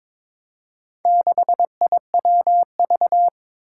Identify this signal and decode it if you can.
Morse code, heard as a keyed tone, about 21 wpm, 700 Hz. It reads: 6IWV
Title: Four Challenging Letter-Number Combinations 22wpm